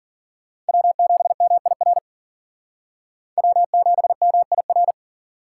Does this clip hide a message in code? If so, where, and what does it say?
Yes: Morse code from 0.7 to 4.9 s, heard as a keyed tone, about 52 wpm, 700 Hz. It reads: W7MIR W7MIR